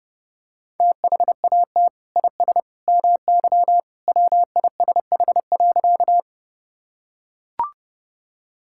Morse code, heard as a keyed tone, about 30 wpm, 700 Hz. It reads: THAT IS MY WISH.
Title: Sentences from Top 1600 Words 30wpm